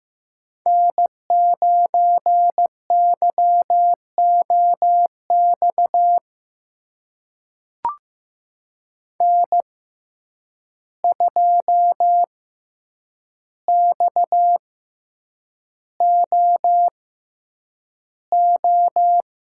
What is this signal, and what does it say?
Morse code, heard as a keyed tone, about 15 wpm, spaced as for 8 wpm, 700 Hz. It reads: N9YOX N2XOO
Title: USA Call Signs - 5 Characters - 6x Character Spacing - 15wpm